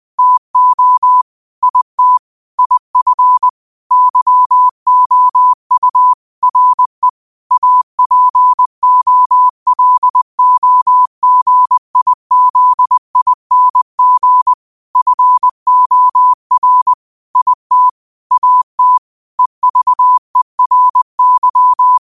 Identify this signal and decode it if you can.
Morse code, heard as a keyed tone, about 20 wpm, 1000 Hz. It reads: TO IT IF YOU RE APOLOGIZING FOR IT AT EVERY
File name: qotdfiles / 20wpm